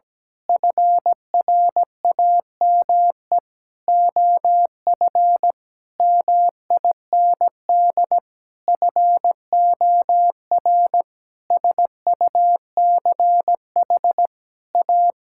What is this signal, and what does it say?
Morse code, heard as a keyed tone, about 17 wpm, 700 Hz. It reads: FRAME OF MIND FOR SUCH A
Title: Sentences from Top 1000 Words - Review Part 6 - 17wpm